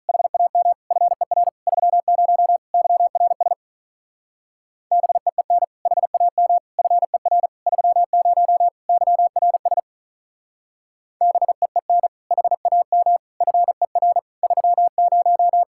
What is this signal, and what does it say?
Morse code, heard as a keyed tone, about 44 wpm, 700 Hz. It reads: HAM FER 30 YRS BEEN HAM FER 30 YRS BEEN HAM FER 30